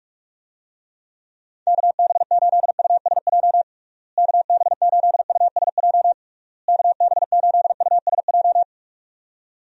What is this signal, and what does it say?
Morse code, heard as a keyed tone, about 46 wpm, 700 Hz. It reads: KB8USJ KB8USJ KB8USJ
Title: USA Call Signs - 6 Characters - Repeated 3x - 45wpm